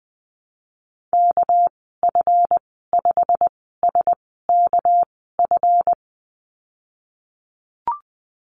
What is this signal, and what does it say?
Morse code, heard as a keyed tone, about 20 wpm, spaced as for 15 wpm, 700 Hz. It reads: KF5SKF
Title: 207 Lesson 33g - Letter j - Call Signs 20wpm